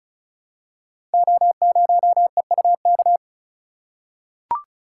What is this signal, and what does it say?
Morse code, heard as a keyed tone, about 36 wpm, 700 Hz. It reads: O0EUK
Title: Code Groups - 5 Letter:Numbers - 35wpm